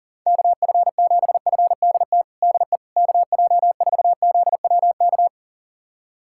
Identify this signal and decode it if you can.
Morse code, heard as a keyed tone, about 40 wpm, 700 Hz. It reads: KF7FDT DE KJ4ZWK
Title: USA Callsign DE Callsign - 6 Characters - 40wpm